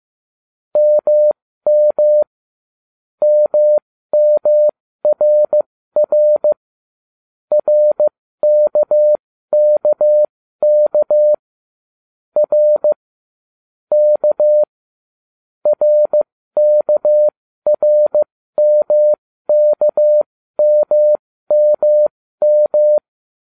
Morse code, heard as a keyed tone, about 15 wpm, spaced as for 13 wpm, 600 Hz. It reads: MM MMRR RKKK R K RKRMKMMM